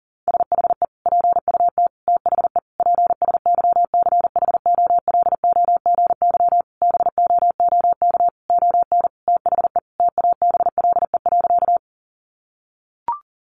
Morse code, heard as a keyed tone, about 40 wpm, 700 Hz. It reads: SHE PUT THE PSYCHOLOGY BOOK ON THE TABLE.
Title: Sentences from Top 1400 Words 40wpm